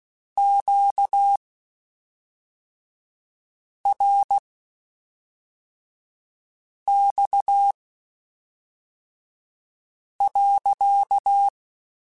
Morse code, heard as a keyed tone, about 16 wpm, spaced as for 3 wpm, 800 Hz. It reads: QRX.
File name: Q Calls / Q Calls-mp3s / Q Calls sppc @5